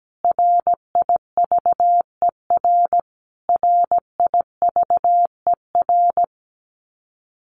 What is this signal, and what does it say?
Morse code, heard as a keyed tone, about 17 wpm, 700 Hz. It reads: RIVER RIVER
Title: Top 300 Words Repeated 17wpm